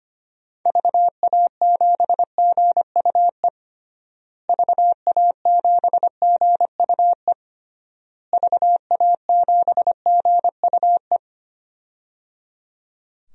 Morse code, heard as a keyed tone, about 25 wpm, 700 Hz. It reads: VA7GUE VA7GUE VA7GUE
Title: DXCC-100 Call Signs - 6 Characters - Repeated 3x - 25wpm